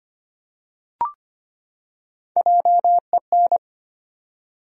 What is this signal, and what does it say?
Morse code, heard as a keyed tone, about 25 wpm, 700 Hz. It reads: JEN
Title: USA Common Names - Max 5 Letters - 25wpm